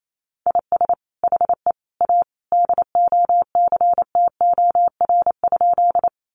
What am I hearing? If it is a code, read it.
Morse code, heard as a keyed tone, about 28 wpm, 700 Hz. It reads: IS HE A DOCTOR?
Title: Sentences from Top 1500 Words 28wpm